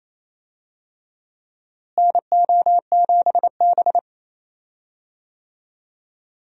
Morse code, heard as a keyed tone, about 28 wpm, 700 Hz. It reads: NO7B
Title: USA Call Signs - 4 Characters - Repeated 2x - 28wpm